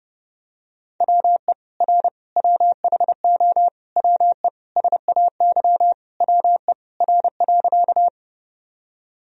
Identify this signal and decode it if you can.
Morse code, heard as a keyed tone, about 30 wpm, 700 Hz. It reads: WE R WHO WE SAY WE R.
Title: Sentences from Top 200 Words and Abbreviations 30wpm